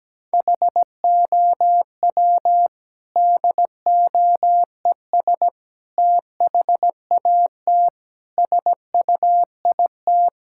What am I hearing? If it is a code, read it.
Morse code, heard as a keyed tone, about 17 wpm, 700 Hz. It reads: HOW DOES THAT SUIT
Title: Sentences from Top 1600 Words 17wpm